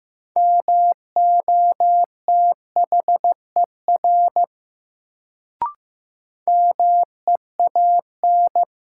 Morse code, heard as a keyed tone, about 15 wpm, 700 Hz. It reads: MOTHER MEAN
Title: Sentences from Top 200 Words - New Word Drill - 15wpm